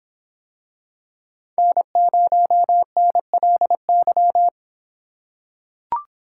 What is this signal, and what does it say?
Morse code, heard as a keyed tone, about 26 wpm, 700 Hz. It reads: N0NLY